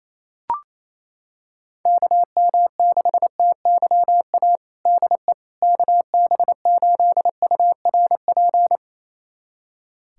Morse code, heard as a keyed tone, about 28 wpm, 700 Hz. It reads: KM6TYA DE KB8URP
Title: USA Callsign DE Callsign 28wpm